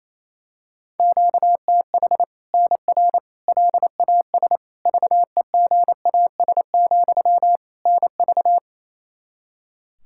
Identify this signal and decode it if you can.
Morse code, heard as a keyed tone, about 28 wpm, 700 Hz. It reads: QTH NR LAS VEGAS, NV
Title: Practice QSO - Exchange 1 - Beginning - 28wpm